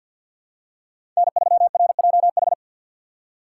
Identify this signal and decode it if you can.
Morse code, heard as a keyed tone, about 47 wpm, 700 Hz. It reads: N3RJH
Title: USA Call Signs - 5 Characters - 4x Character Spacing - 50wpm